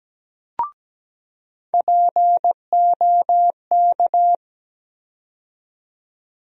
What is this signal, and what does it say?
Morse code, heard as a keyed tone, about 17 wpm, 700 Hz. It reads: POK
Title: USA Call Sign Postfixes - 3 Letters 17wpm